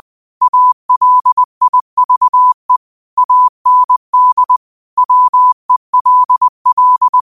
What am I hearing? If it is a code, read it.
Morse code, heard as a keyed tone, about 20 wpm, 1000 Hz. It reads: ALIVE AND WELL